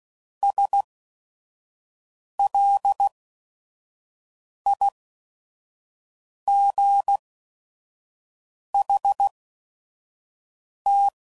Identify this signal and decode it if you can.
Morse code, heard as a keyed tone, about 16 wpm, spaced as for 5 wpm, 800 Hz. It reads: SLIGHT
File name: CW Code Groups @5WPM / Weather Sets / Weather 4 17